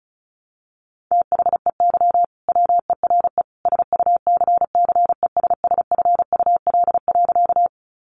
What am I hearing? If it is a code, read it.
Morse code, heard as a keyed tone, about 34 wpm, 700 Hz. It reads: THEY WERE SUCCESSFUL.
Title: Sentences from Top 900 Words 35wpm